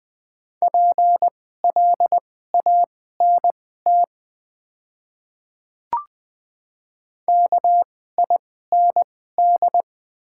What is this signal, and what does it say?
Morse code, heard as a keyed tone, about 20 wpm, spaced as for 15 wpm, 700 Hz. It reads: PLANT KIND
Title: 2x Character Spacing - Top 300 Words - 20wpm